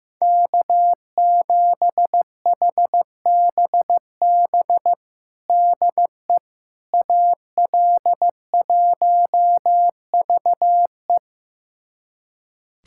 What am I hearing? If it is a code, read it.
Morse code, heard as a keyed tone, about 16 wpm, 700 Hz. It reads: K7HBB DE AL1VE
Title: USA Callsign DE Callsign - 5 Characters - 15wpm